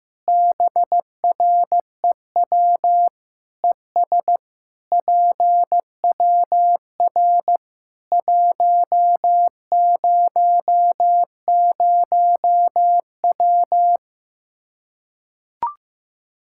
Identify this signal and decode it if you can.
Morse code, heard as a keyed tone, about 15 wpm, 700 Hz. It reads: BREW ES PWR 100W
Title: Practice QSO - Exchange 2 - Info - 15wpm